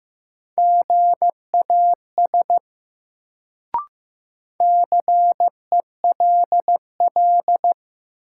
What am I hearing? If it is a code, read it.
Morse code, heard as a keyed tone, about 15 wpm, 700 Hz. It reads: GAS CELL
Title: Sentences from Top 1100 Words - New Word Drill - 15wpm